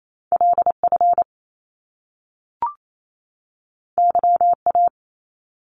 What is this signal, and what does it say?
Morse code, heard as a keyed tone, about 28 wpm, 700 Hz. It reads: LF YA
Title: Top Two Letter Combinations in Words 28wpm